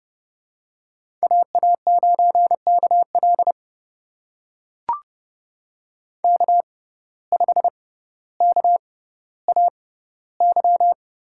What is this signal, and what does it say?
Morse code, heard as a keyed tone, about 30 wpm, spaced as for 15 wpm, 700 Hz. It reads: AA9KL K5KAY
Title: USA Call Signs - 5 Characters - 6x Character Spacing - 30wpm